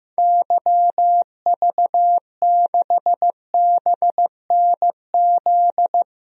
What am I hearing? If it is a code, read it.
Morse code, heard as a keyed tone, about 15 wpm, 700 Hz. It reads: YV6BNZ
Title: DXCC-100 Call Signs - 6 Characters - 15wpm